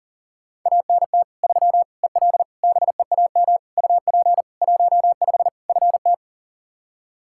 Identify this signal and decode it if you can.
Morse code, heard as a keyed tone, about 37 wpm, 700 Hz. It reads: ANT 3 EL BEAM UP 15 FT
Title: Practice QSO - Exchange 2 - Info - 40wpm